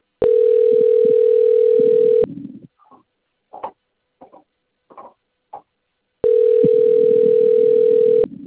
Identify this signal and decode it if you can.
Ringback tone (North America), 440 + 480 Hz.
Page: Americana 2